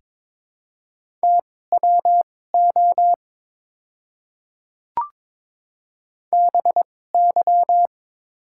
Morse code, heard as a keyed tone, about 22 wpm, spaced as for 16 wpm, 690 Hz. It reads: TWO BY
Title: 2x Character Spacing - Top 100 Words - 22wpm